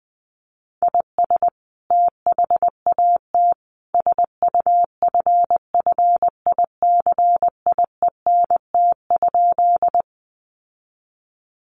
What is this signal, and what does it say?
Morse code, heard as a keyed tone, about 20 wpm, 700 Hz. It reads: IS THAT SUFFICIENT?
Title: Sentences from Top 1400 Words 20wpm